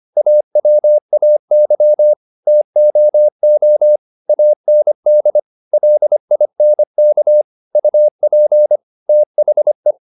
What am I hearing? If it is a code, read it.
Morse code, heard as a keyed tone, about 25 wpm, 600 Hz. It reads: AWAY TOO AND LINK UP THE
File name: the old man and the sea part 1 25wpm